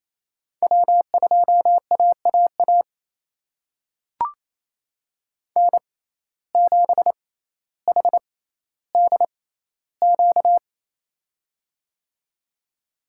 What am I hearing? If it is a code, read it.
Morse code, heard as a keyed tone, about 27 wpm, spaced as for 14 wpm, 700 Hz. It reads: W2AAA N7HDQ
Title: USA Call Signs - 5 Characters - 6x Character Spacing - 28wpm